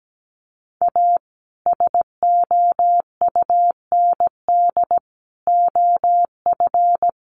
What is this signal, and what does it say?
Morse code, heard as a keyed tone, about 17 wpm, 700 Hz. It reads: A SOUND OF
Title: A Sound Of Thunder 17wpm